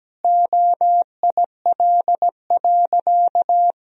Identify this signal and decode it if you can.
Morse code, heard as a keyed tone, about 17 wpm, 700 Hz. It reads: OIL.
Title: Sentences from Top 1000 Words - Review Part 4 - 17wpm